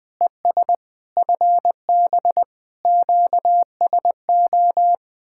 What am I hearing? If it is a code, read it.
Morse code, heard as a keyed tone, about 20 wpm, 700 Hz. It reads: ES FB QSO
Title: Speed-Racing Common QSO Phrases Encore - 17wpm